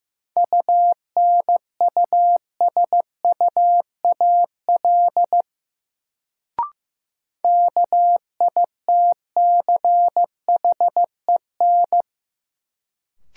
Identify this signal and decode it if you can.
Morse code, heard as a keyed tone, about 15 wpm, 700 Hz. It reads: UNUSUAL KITCHEN
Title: Sentences from Top 1400 Words - New Word Drill Alt - 15wpm